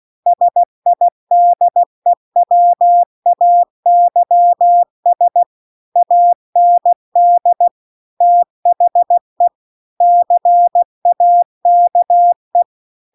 Morse code, heard as a keyed tone, about 16 wpm, 700 Hz. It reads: SIDEWAYS AND THE CAKE
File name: CW-The-Big-Twoheaded-River-2-16WPM